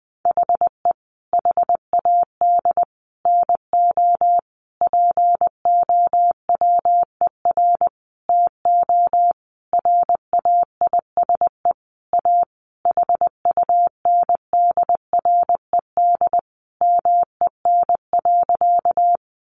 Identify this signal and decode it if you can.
Morse code, heard as a keyed tone, about 20 wpm, 700 Hz. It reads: HE HAD NO POWER TO RAISE A HUNDRED MEN.
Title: Sentences from Top 1400 Words 20wpm